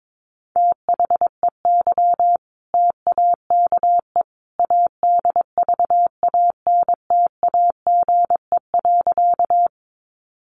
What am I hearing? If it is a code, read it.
Morse code, heard as a keyed tone, about 22 wpm, 700 Hz. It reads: THEY TAKE ADVANTAGE.
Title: Sentences from Top 900 Words 22wpm